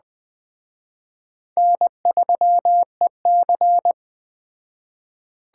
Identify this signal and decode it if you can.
Morse code, heard as a keyed tone, about 20 wpm, 700 Hz. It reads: N3EC